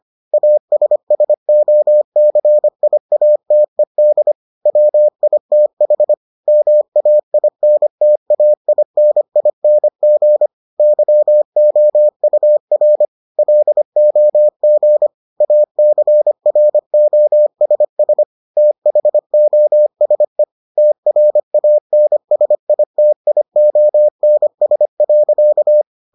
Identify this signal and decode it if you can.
Morse code, heard as a keyed tone, about 25 wpm, 600 Hz. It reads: ASSOCIATED WITH MAINTAINING YOUR LOG ACROSS THOSE TRANSITIONS.